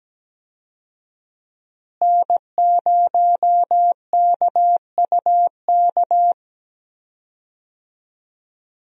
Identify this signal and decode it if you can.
Morse code, heard as a keyed tone, about 17 wpm, 700 Hz. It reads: N0KUK
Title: USA Call Signs - 5 Characters - Repeated 2x - 17wpm